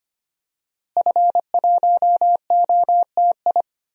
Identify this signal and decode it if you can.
Morse code, heard as a keyed tone, about 25 wpm, 700 Hz. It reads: F1OTI
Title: DXCC-100 Call Signs - 5 Characters - Repeated 2x - 25wpm